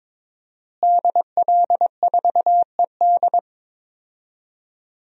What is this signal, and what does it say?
Morse code, heard as a keyed tone, about 22 wpm, 700 Hz. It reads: DL4ED
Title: DXCC-100 Call Signs - 5 Characters - Repeated 2x - 22wpm